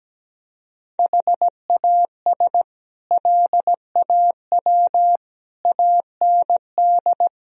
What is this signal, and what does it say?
Morse code, heard as a keyed tone, about 17 wpm, 700 Hz. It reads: HAS LAW AND
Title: Sentences from Top 1000 Words - Review Part 3 - 17wpm